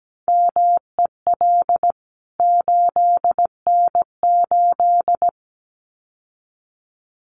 Morse code, heard as a keyed tone, about 17 wpm, 700 Hz. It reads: MEL 8N8
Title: CWT Exchange 17wpm